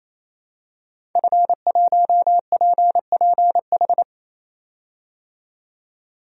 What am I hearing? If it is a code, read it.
Morse code, heard as a keyed tone, about 28 wpm, 700 Hz. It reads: F1PPH
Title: DXCC-100 Call Signs - 5 Characters - 28wpm